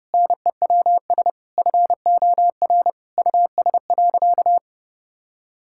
Morse code, heard as a keyed tone, about 32 wpm, 700 Hz. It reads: NEWS FOR US.
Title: Sentences from Top 1000 Words - Review Part 5 - 30wpm